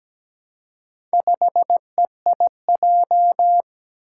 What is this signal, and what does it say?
Morse code, heard as a keyed tone, about 17 wpm, 700 Hz. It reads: 5EIJ
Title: Four Challenging Letter-Number Combinations 17wpm